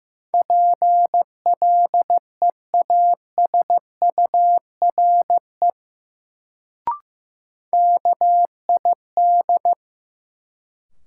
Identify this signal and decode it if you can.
Morse code, heard as a keyed tone, about 15 wpm, 700 Hz. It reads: PLEASURE KID
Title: Sentences from Top 1600 Words - New Word Drill - 15wpm